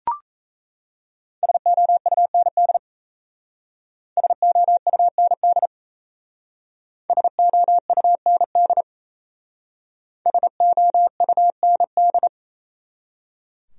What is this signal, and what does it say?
Morse code, heard as a keyed tone, about 34 wpm, 700 Hz. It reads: SOUND SOUND SOUND SOUND
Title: Speed-Racing Top 300 Words 28wpm